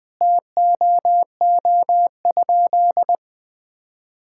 Morse code, heard as a keyed tone, about 20 wpm, 700 Hz. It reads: TOO?
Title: Sentences from Top 1000 Words - Review Part 6 - 20wpm